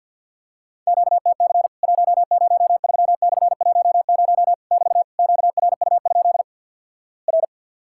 Morse code, heard as a keyed tone, about 50 wpm, 700 Hz. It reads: QTC 103/10 <BT> QRU? R
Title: Worked All Europe DX Contest - QTC Exchange - 50wpm